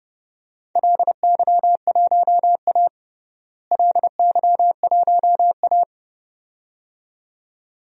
Morse code, heard as a keyed tone, about 30 wpm, 700 Hz. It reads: LY1A LY1A
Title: DXCC-100 Call Signs - 4 Characters - Repeated 2x - 30wpm